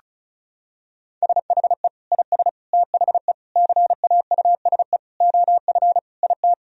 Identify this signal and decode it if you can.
Morse code, heard as a keyed tone, about 36 wpm, 700 Hz. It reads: SHE IS THE CAUSE OF IT
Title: Sentences from Top 1000 Words - Review Part 1 - 35wpm